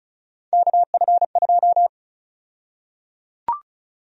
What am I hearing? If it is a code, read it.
Morse code, heard as a keyed tone, about 36 wpm, 700 Hz. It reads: KF2
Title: USA Call Sign Prefixes - Repeated 2x - 35wpm